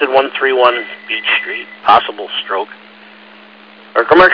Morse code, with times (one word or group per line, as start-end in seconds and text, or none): none